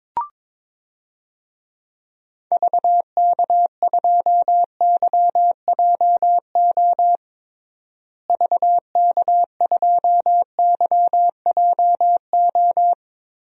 Morse code, heard as a keyed tone, about 22 wpm, 700 Hz. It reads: VK2YJO VK2YJO
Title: DXCC-100 Call Signs - 6 Characters - Repeated 2x - 22wpm